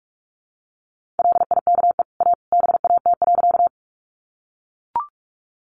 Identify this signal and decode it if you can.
Morse code, heard as a keyed tone, about 44 wpm, 700 Hz. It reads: LIKE A BAT.
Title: Sentences from Top 1500 Words 45wpm